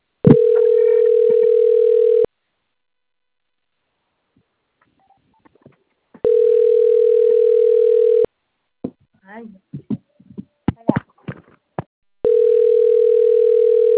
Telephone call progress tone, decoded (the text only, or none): ringback tone